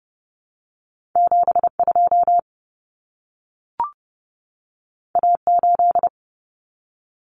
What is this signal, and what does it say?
Morse code, heard as a keyed tone, about 30 wpm, 700 Hz. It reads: 72 A8
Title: Common Cut Numbers - 2-Digit 30wpm